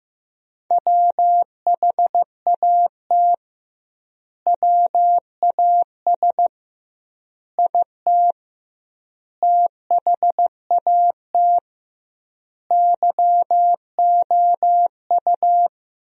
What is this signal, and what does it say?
Morse code, heard as a keyed tone, about 15 wpm, 700 Hz. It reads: WHAT WAS IT THAT YOU
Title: 2x Word-Spacing - Sentences from Top 300 Words - 15wpm